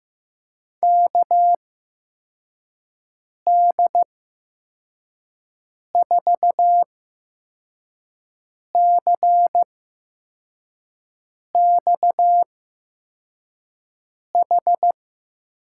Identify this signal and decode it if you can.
Morse code, heard as a keyed tone, about 15 wpm, spaced as for 4 wpm, 700 Hz. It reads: KD4CXH
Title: USA Call Signs - 8x Character Spacing - 15wpm